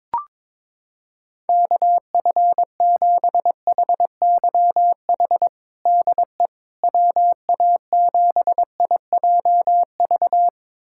Morse code, heard as a keyed tone, about 21 wpm, 700 Hz. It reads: KF7HYH DE WA7IJV
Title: USA Callsign DE Callsign - 6 Characters - 22wpm